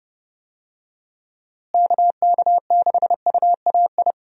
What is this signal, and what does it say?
Morse code, heard as a keyed tone, about 30 wpm, 700 Hz. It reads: KK6UAI
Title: USA Call Signs - 6 Characters - Repeated 3x - 30wpm